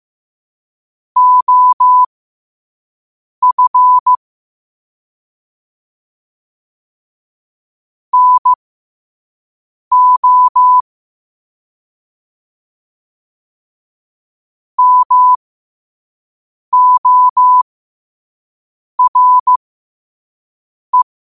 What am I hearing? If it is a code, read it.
Morse code, heard as a keyed tone, about 15 wpm, spaced as for 5 wpm, 1000 Hz. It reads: OF NO MORE
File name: qotdfiles / 5wpm